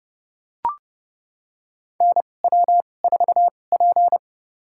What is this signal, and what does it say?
Morse code, heard as a keyed tone, about 31 wpm, spaced as for 22 wpm, 700 Hz. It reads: NW4P
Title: USA Call Signs - 4 Characters - 2x Character Spacing - 30wpm